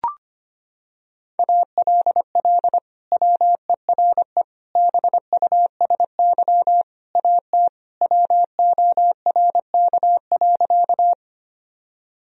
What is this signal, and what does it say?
Morse code, heard as a keyed tone, about 25 wpm, 700 Hz. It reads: ALL WERE BUSY AT WORK.
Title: Sentences from Top 1000 Words - Review Part 3 - 25wpm